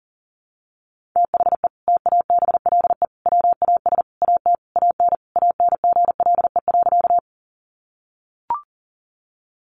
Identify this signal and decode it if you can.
Morse code, heard as a keyed tone, about 40 wpm, 700 Hz. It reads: THE TABLE WAS AT AN ANGLE.
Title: Sentences from Top 1500 Words 40wpm